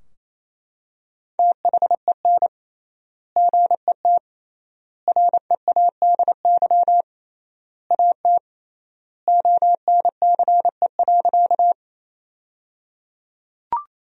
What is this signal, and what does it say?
Morse code, heard as a keyed tone, about 28 wpm, 700 Hz. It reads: THEN GET READY AT ONCE.
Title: 3x Word-Spacing - Sentences from Top 400 Words - 28wpm